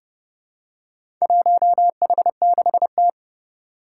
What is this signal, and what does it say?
Morse code, heard as a keyed tone, about 30 wpm, 700 Hz. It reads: 1H6T
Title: Four Challenging Letter-Number Combinations 30wpm